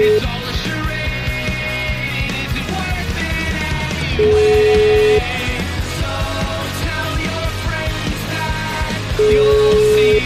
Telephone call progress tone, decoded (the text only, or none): ringback tone